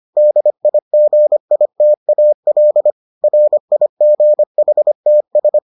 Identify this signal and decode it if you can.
Morse code, heard as a keyed tone, about 25 wpm, 600 Hz. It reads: DIGITAL RIGHTS